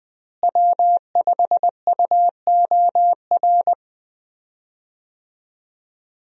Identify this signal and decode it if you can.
Morse code, heard as a keyed tone, about 20 wpm, 700 Hz. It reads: W5UOR
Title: USA Call Signs - 5 Characters - Repeated 3x - 20wpm